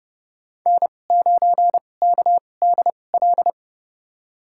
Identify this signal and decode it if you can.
Morse code, heard as a keyed tone, about 30 wpm, spaced as for 22 wpm, 700 Hz. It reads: N9KDL
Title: USA Call Signs - 5 Characters - 2x Character Spacing - 30wpm